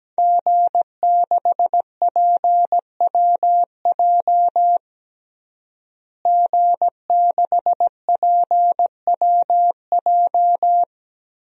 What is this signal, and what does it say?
Morse code, heard as a keyed tone, about 17 wpm, 700 Hz. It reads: G6PWJ G6PWJ
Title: DXCC-100 Call Signs - 5 Characters - Repeated 3x - 17wpm